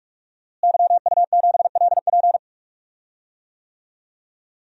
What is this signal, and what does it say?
Morse code, heard as a keyed tone, about 43 wpm, 700 Hz. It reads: YU7LP
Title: DXCC-100 Call Signs - 5 Characters - Repeated 2x - 45wpm